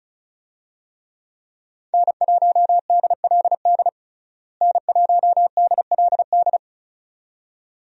Morse code, heard as a keyed tone, about 34 wpm, 700 Hz. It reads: N1DLD N1DLD
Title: USA Call Signs - 5 Characters - Repeated 2x - 35wpm